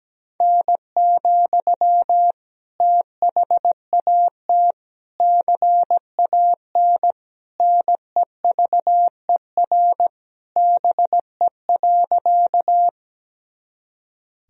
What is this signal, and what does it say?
Morse code, heard as a keyed tone, about 17 wpm, 700 Hz. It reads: N, THAT CAN NEVER BE.
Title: Sentences from Top 300 Words and Abbreviations 17wpm